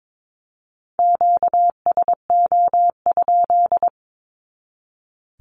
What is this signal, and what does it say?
Morse code, heard as a keyed tone, about 22 wpm, 700 Hz. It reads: QSO?
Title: Q-Codes 22wpm